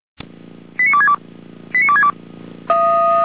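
Five-tone selective call (ZVEI), heard as quick runs of five tones, tones of about 70 ms. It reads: 89262 89262
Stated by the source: ZVEI five-tone selective call 88564 received by SDR.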